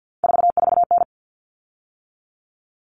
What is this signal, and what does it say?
Morse code, heard as a keyed tone, about 48 wpm, 700 Hz. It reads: 44N
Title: Common Cut Numbers - 3-Digit 50wpm